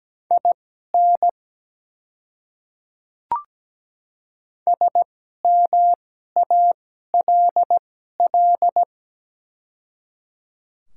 Morse code, heard as a keyed tone, about 17 wpm, spaced as for 12 wpm, 700 Hz. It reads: IN SMALL
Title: 2x Character Spacing - Top 200 Words - 17wpm